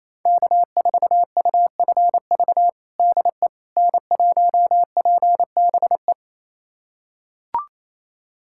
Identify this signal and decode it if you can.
Morse code, heard as a keyed tone, about 28 wpm, 700 Hz. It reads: K4UFV DE N1PBE
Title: USA Callsign DE Callsign - 5 Characters - 28wpm